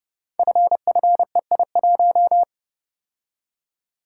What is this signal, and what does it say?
Morse code, heard as a keyed tone, about 28 wpm, 700 Hz. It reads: FFEI1
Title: Code Groups - 5 Letter:Numbers - 30wpm